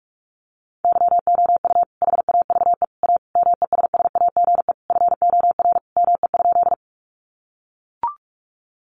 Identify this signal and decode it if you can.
Morse code, heard as a keyed tone, about 44 wpm, 700 Hz. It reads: YOU HAVE A MESSAGE FOR ME?
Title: Sentences from Top 1100 Words 45wpm